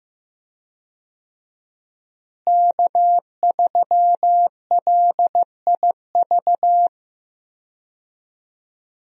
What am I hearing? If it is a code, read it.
Morse code, heard as a keyed tone, about 15 wpm, 700 Hz. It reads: K3LIV
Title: USA Call Signs - 5 Characters - Repeated 3x - 15wpm